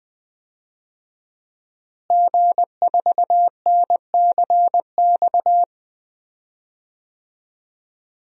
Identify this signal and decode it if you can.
Morse code, heard as a keyed tone, about 20 wpm, 700 Hz. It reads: G4NCX
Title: DXCC-100 Call Signs - 5 Characters - Repeated 2x - 20wpm